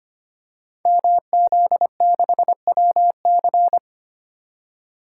Morse code, heard as a keyed tone, about 25 wpm, 700 Hz. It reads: MZ6WC
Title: Code Groups - 5 Letter:Numbers - 25wpm